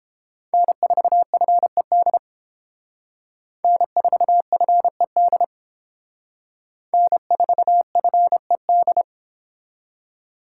Morse code, heard as a keyed tone, about 29 wpm, 700 Hz. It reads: N4FED N4FED N4FED